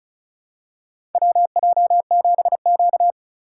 Morse code, heard as a keyed tone, about 36 wpm, 700 Hz. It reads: WJ7Q
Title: USA Call Signs - 4 Characters - 2x Character Spacing - 35wpm